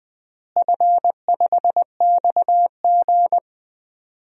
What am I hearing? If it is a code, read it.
Morse code, heard as a keyed tone, about 20 wpm, 700 Hz. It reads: F5XG
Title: DXCC-100 Call Signs - 4 Characters - Repeated 2x - 20wpm